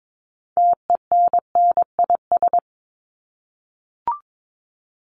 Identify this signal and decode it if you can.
Morse code, heard as a keyed tone, about 22 wpm, 700 Hz. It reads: TENNIS
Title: Top 1501-1750 Words 22wpm